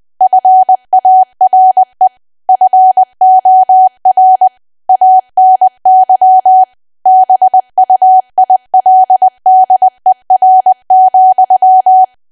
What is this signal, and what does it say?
Morse code, heard as a keyed tone, about 20 wpm, 750 Hz. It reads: FARE FOR ANY BUILDER,
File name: cwa / W1AW-Level3 / 20WPM